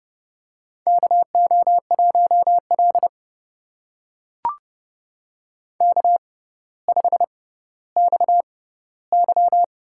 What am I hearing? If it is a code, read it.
Morse code, heard as a keyed tone, about 30 wpm, spaced as for 15 wpm, 700 Hz. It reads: KO1L K5XY
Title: USA Call Signs - 4 Characters - 6x Character Spacing - 30wpm